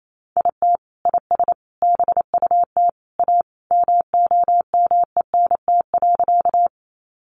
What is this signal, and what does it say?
Morse code, heard as a keyed tone, about 28 wpm, 700 Hz. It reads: IT IS BUT A MOMENT.
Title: Sentences from Top 1000 Words 28wpm